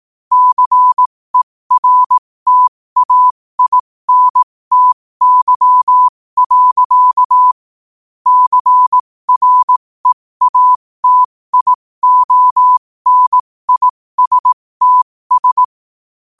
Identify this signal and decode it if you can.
Morse code, heard as a keyed tone, about 18 wpm, spaced as for 15 wpm, 1000 Hz. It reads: CERTAINTY. CREATIONISTS
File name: 15wpm